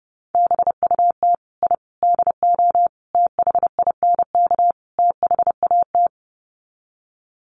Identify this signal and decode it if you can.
Morse code, heard as a keyed tone, about 30 wpm, 700 Hz. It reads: BUT I DO THINK THAT
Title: Sets of 5 Words Encore2 30wpm